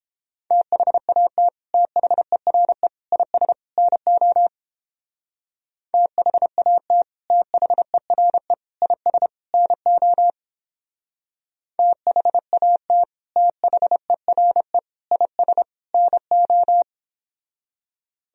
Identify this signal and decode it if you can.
Morse code, heard as a keyed tone, about 29 wpm, 700 Hz. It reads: THAT THERE IS NO THAT THERE IS NO THAT THERE IS NO